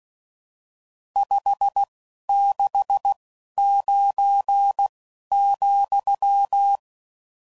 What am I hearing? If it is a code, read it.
Morse code, heard as a keyed tone, about 16 wpm, spaced as for 11 wpm, 800 Hz. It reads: 569,
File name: CW Code Groups @10WPM / RSTs / Various RST No's